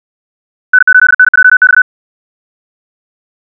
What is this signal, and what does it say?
Morse code, heard as a keyed tone, about 64 wpm, 1500 Hz. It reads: N8NJX
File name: ja7uhv / SampleHST3